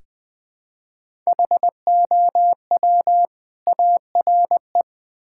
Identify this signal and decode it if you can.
Morse code, heard as a keyed tone, about 20 wpm, 700 Hz. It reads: HOW ARE
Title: Build A Phrase - 5-Word Phrases - 20wpm